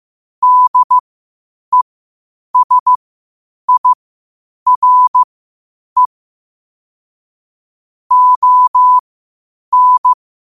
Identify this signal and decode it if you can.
Morse code, heard as a keyed tone, about 15 wpm, spaced as for 8 wpm, 1000 Hz. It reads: DESIRE ON
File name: qotdfiles / 8wpm